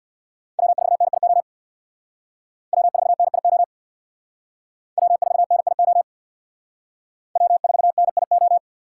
Morse code, heard as a keyed tone, about 61 wpm, 700 Hz. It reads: W4NIO W4NIO W4NIO W4NIO
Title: Speed Racing - USA Call Signs 50wpm